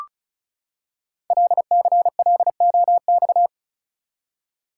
Morse code, heard as a keyed tone, about 36 wpm, 700 Hz. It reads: LCLOX
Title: Code Groups - 5 Letter:Numbers - 35wpm